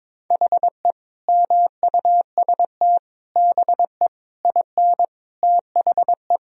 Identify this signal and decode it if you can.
Morse code, heard as a keyed tone, about 21 wpm, 700 Hz. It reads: HE MUST BE IN THE
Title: Sentences from Top 1000 Words - Review Part 2 - 22wpm